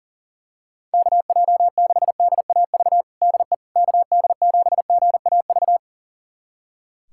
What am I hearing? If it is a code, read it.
Morse code, heard as a keyed tone, about 40 wpm, 700 Hz. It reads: KJ6DAV DE KD7GAV
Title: USA Callsign DE Callsign - 6 Characters - 40wpm